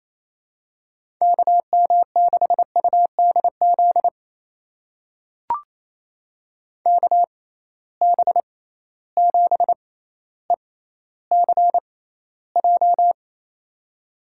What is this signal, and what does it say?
Morse code, heard as a keyed tone, about 28 wpm, spaced as for 14 wpm, 700 Hz. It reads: KM6UDZ KB7ECJ
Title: USA Call Signs - 6 Characters - 6x Character Spacing - 28wpm